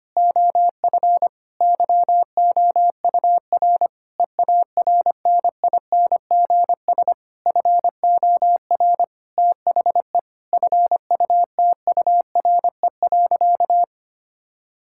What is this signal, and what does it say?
Morse code, heard as a keyed tone, about 25 wpm, 700 Hz. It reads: OF YOUR EARNINGS FOR THE FUTURE.
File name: practice / 25wpm / Difficult Sentences with Hints - 25wpm